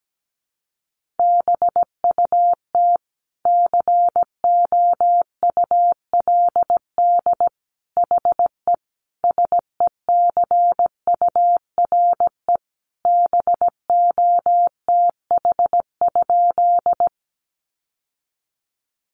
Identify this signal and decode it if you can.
Morse code, heard as a keyed tone, about 17 wpm, 700 Hz. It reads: BUT COULD HE SECURE BOTH?
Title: Sentences from Top 1200 Words 17wpm